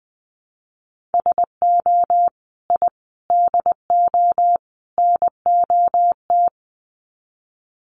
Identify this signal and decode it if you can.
Morse code, heard as a keyed tone, about 20 wpm, 700 Hz. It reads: SO I DO NOT
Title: Sets of 4 Words 20wpm